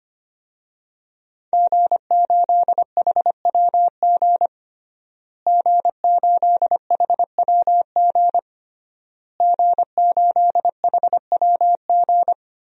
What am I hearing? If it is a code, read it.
Morse code, heard as a keyed tone, about 25 wpm, 700 Hz. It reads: G8HWG G8HWG G8HWG
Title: DXCC-100 Call Signs - 5 Characters - Repeated 3x - 25wpm